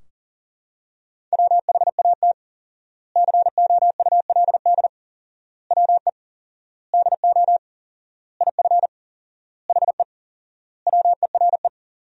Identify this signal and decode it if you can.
Morse code, heard as a keyed tone, about 37 wpm, 700 Hz. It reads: WHAT COULD WE DO IF HE WERE
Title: 4x Word-Spacing - Sentences from Top 400 Words - 40wpm